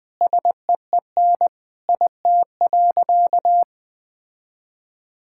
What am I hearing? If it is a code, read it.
Morse code, heard as a keyed tone, about 20 wpm, 700 Hz. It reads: SEEN IT.
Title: Sentences with 3-Word Phrases - 20wpm